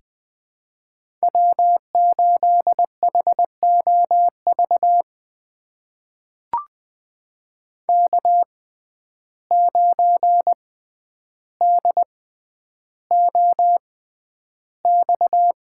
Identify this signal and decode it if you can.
Morse code, heard as a keyed tone, about 21 wpm, spaced as for 10 wpm, 700 Hz. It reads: W8HOV K9DOX
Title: USA Call Signs - 5 Characters - 6x Character Spacing - 20wpm